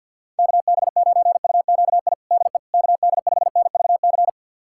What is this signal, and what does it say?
Morse code, heard as a keyed tone, about 52 wpm, 700 Hz. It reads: KB9UQI DE KD5NVC
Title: USA Callsign DE Callsign - 6 Characters - 50wpm